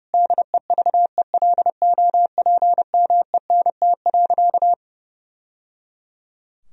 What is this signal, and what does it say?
Morse code, heard as a keyed tone, about 30 wpm, 700 Hz. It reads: DEVELOPMENT.
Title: Sentences from Top 1000 Words - Review Part 4 - 30wpm